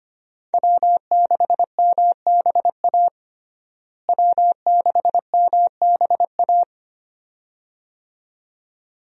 Morse code, heard as a keyed tone, about 25 wpm, 700 Hz. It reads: W6MBA W6MBA
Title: USA Call Signs - 5 Characters - Repeated 2x - 25wpm